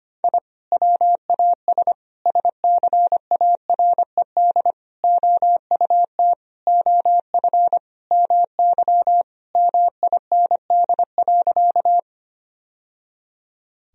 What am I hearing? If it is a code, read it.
Morse code, heard as a keyed tone, about 25 wpm, 700 Hz. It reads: I WAS SCARED OUT OF MY MIND.
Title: Sentences from Top 1000 Words - Review Part 4 - 25wpm